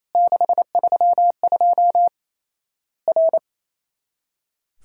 Morse code, heard as a keyed tone, about 27 wpm, 700 Hz. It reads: 632 R
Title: Worked All Europe DX Contest - QTC Exchange - 28wpm